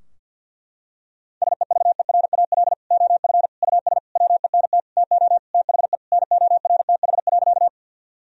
Morse code, heard as a keyed tone, about 47 wpm, 700 Hz. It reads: SEVERAL OF US WENT TO THE NORTH.
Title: Sentences from Top 1000 Words - Review Part 2 - 50wpm